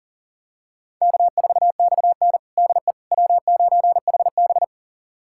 Morse code, heard as a keyed tone, about 37 wpm, 700 Hz. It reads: K4XN DE W9HB
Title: USA Callsign DE Callsign - 4 Characters - 40wpm